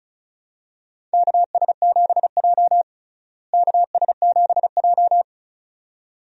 Morse code, heard as a keyed tone, about 34 wpm, 700 Hz. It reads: KS7J KS7J
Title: USA Call Signs - 4 Characters - Repeated 2x - 35wpm